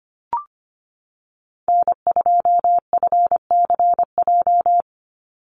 Morse code, heard as a keyed tone, about 25 wpm, 700 Hz. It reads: N2FCJ
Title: USA Call Signs 25wpm